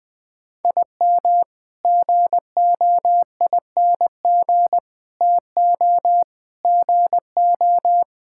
Morse code, heard as a keyed tone, about 20 wpm, 700 Hz. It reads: IM GOING TO GO
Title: Sentences with 8-Word Phrases - Spoken Beginning - 20wpm